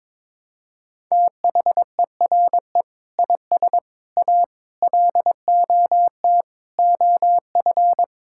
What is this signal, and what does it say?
Morse code, heard as a keyed tone, about 22 wpm, 700 Hz. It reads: THERE IS A LOT OF
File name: practice / 22wpm / Sentences with 2-Word Phrases - 22wpm